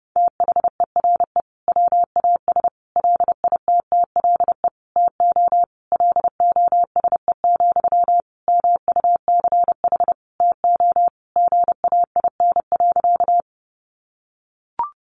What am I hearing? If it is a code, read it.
Morse code, heard as a keyed tone, about 30 wpm, 700 Hz. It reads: THERE WAS LITTLE TO LOSE, MUCH TO GAIN.
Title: Sentences from Top 1100 Words 30wpm